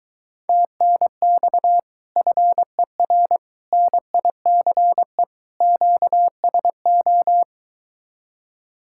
Morse code, heard as a keyed tone, about 23 wpm, 700 Hz. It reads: TNX FER NICE QSO